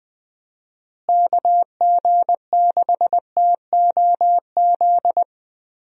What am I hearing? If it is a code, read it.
Morse code, heard as a keyed tone, about 20 wpm, 700 Hz. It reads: KG6TOZ
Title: USA Call Signs - 6 Characters - Repeated 2x - 20wpm